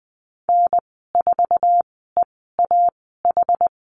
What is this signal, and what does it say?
Morse code, heard as a keyed tone, about 20 wpm, spaced as for 14 wpm, 700 Hz. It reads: N4EAH
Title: 076 Lesson 13g - Letter L - Call Signs 20wpm